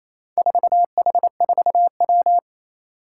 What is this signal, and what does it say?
Morse code, heard as a keyed tone, about 29 wpm, 700 Hz. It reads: 4H4W
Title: Four Challenging Letter-Number Combinations 28wpm